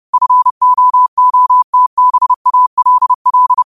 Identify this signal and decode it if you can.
Morse code, heard as a keyed tone, about 30 wpm, 1000 Hz. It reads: FOOTBALL
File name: qotdfiles / 30wpm